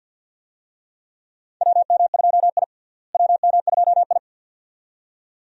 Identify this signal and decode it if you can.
Morse code, heard as a keyed tone, about 52 wpm, 700 Hz. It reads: WM2I WM2I
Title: USA Call Signs - 4 Characters - Repeated 2x - 50wpm